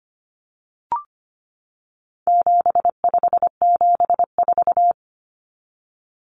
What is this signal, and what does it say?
Morse code, heard as a keyed tone, about 25 wpm, 700 Hz. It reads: 7574
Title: Common Cut Numbers - 4-Digit 25wpm